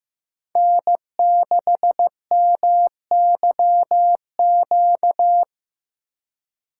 Morse code, heard as a keyed tone, about 19 wpm, 700 Hz. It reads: N6MYQ